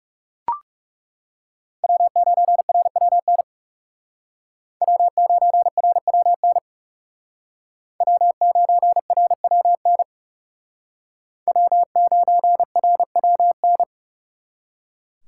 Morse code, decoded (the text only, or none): W9RWN W9RWN W9RWN W9RWN